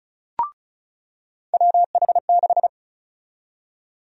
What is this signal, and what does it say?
Morse code, heard as a keyed tone, about 36 wpm, 700 Hz. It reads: WH6
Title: USA Call Sign Prefixes - 35wpm